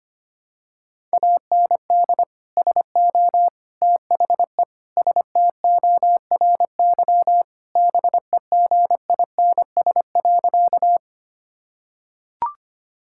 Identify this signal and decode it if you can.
Morse code, heard as a keyed tone, about 25 wpm, 700 Hz. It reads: AND SO THE STORY BEGINS.
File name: practice / 25wpm / Sentences with 2-Word Phrases - 25wpm